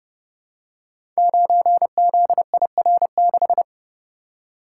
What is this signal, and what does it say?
Morse code, heard as a keyed tone, about 30 wpm, 700 Hz. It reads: 9ZIR6
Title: Code Groups - 5 Letter:Numbers - 30wpm